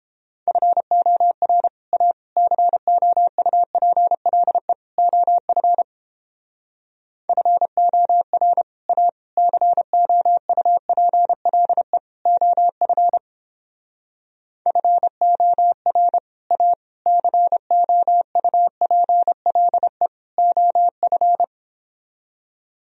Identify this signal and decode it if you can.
Morse code, heard as a keyed tone, about 31 wpm, 700 Hz. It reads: FOR A COUPLE OF FOR A COUPLE OF FOR A COUPLE OF